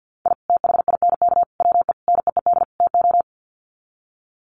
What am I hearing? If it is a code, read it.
Morse code, heard as a keyed tone, about 49 wpm, 700 Hz. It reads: I THINK WE NEED TO
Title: Sets of 5 Words 50wpm